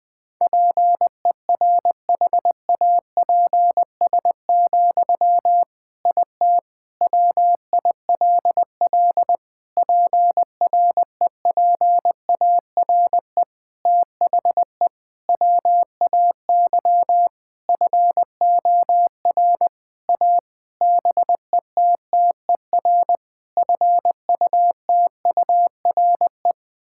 Morse code, heard as a keyed tone, about 20 wpm, 700 Hz. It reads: PERHAPS, IT WILL PREPARE THE WAY FOR A BETTER FUTURE
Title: Sentences from Top 1000 Words - Review Part 6 - 20wpm